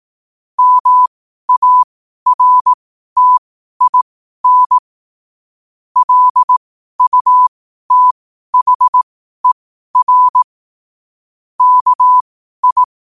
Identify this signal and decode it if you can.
Morse code, heard as a keyed tone, about 18 wpm, spaced as for 12 wpm, 1000 Hz. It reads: MARTIN LUTHER KI
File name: qotdfiles / 12wpm